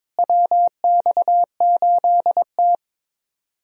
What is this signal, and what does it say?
Morse code, heard as a keyed tone, about 22 wpm, 700 Hz. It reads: WX8T
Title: USA Call Signs - 4 Characters - Repeated 3x - 22wpm